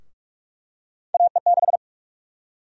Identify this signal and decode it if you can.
Morse code, heard as a keyed tone, about 44 wpm, 700 Hz. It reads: AE6
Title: USA Call Sign Prefixes - Repeated 2x - 45wpm